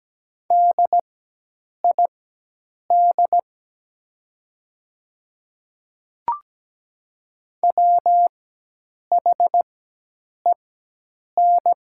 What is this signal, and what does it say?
Morse code, heard as a keyed tone, about 17 wpm, spaced as for 8 wpm, 700 Hz. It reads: DID WHEN
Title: 4x Character Spacing - Top 500 Words - 17wpm